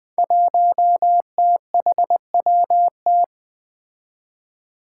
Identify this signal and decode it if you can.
Morse code, heard as a keyed tone, about 20 wpm, 700 Hz. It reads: 1THWT
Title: Five Challenging Letter-Number Combinations 20wpm